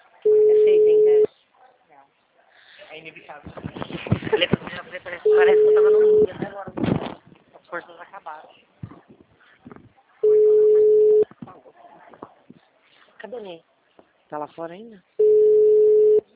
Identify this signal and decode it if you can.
Ringback tone (Europe, 425 Hz), with speech.